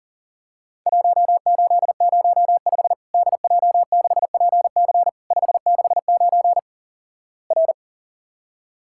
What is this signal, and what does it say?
Morse code, heard as a keyed tone, about 44 wpm, 700 Hz. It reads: 1805 DJ6PC 569 R
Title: Worked All Europe DX Contest - QTC Exchange - 40wpm